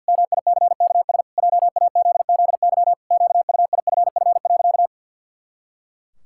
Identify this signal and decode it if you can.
Morse code, heard as a keyed tone, about 50 wpm, 700 Hz. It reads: MICKS JAZZY QUIFF.
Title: Pangram Challenge v2 - 50wpm